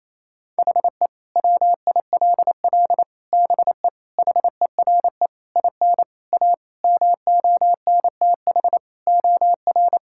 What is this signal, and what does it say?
Morse code, heard as a keyed tone, about 28 wpm, 700 Hz. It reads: HE WILL BE HERE IN A MONTH OR
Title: Sentences from Top 1000 Words - Review Part 5 - 28wpm